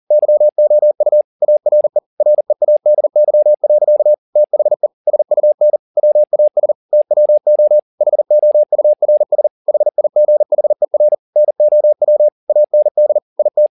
Morse code, heard as a keyed tone, about 40 wpm, 600 Hz. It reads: YOU ARE READY. THE SUN WAS TWO HOURS HIGHER NOW AND IT